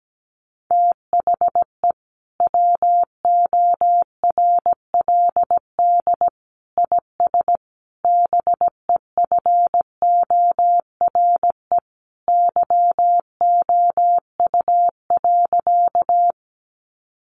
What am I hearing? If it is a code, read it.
Morse code, heard as a keyed tone, about 17 wpm, 700 Hz. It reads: THE WORLD IS BEFORE YOU.
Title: Sentences from Top 200 Words 17wpm